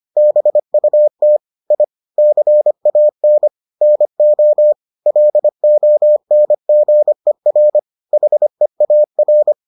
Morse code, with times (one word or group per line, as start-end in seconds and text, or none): 0.2-1.4 BUT
1.7-1.8 I
2.2-3.5 CAN
3.8-4.7 NO
5.1-7.8 LONGER
8.1-9.5 HEAR